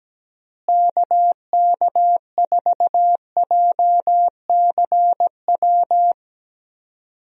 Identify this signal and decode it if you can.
Morse code, heard as a keyed tone, about 17 wpm, 700 Hz. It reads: KK4JCW
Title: USA Call Signs - 6 Characters - Repeated 2x - 17wpm